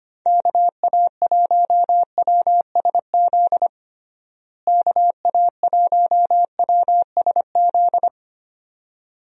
Morse code, heard as a keyed tone, about 25 wpm, 700 Hz. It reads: KA1WSZ KA1WSZ
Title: USA Call Signs - 6 Characters - Repeated 2x - 25wpm